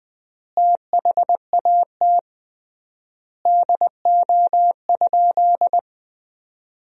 Morse code, heard as a keyed tone, about 21 wpm, 700 Hz. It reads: THAT DO?
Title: 3x Word-Spacing - Sentences from Top 100 Words - 20wpm